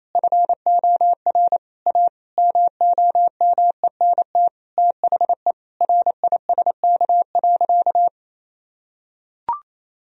Morse code, heard as a keyed tone, about 28 wpm, 700 Hz. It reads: FOR A MOMENT THE RISK.
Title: Sentences from Top 1000 Words - Review Part 4 - 28wpm